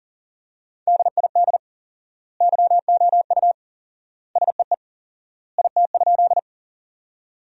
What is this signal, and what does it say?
Morse code, heard as a keyed tone, about 40 wpm, 700 Hz. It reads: DID YOU SEE IT?
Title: 4x Word-Spacing - Sentences from Top 100 Words - 40wpm